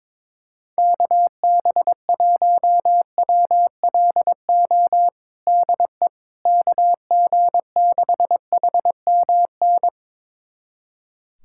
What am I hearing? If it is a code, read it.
Morse code, heard as a keyed tone, about 22 wpm, 700 Hz. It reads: KB1WLO DE KG6HMN
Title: USA Callsign DE Callsign 22wpm